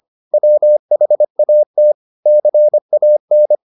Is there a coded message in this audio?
Yes: Morse code from 0.3 to 3.5 s, heard as a keyed tone, about 25 wpm, 600 Hz. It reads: WHAT CAN